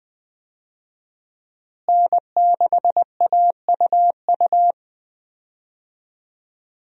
Morse code, heard as a keyed tone, about 20 wpm, 700 Hz. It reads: N6AUU
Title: USA Call Signs - 5 Characters - Repeated 3x - 20wpm